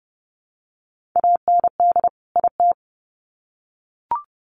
Morse code, heard as a keyed tone, about 30 wpm, 700 Hz. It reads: AND IT
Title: Sets of 2 Words 30wpm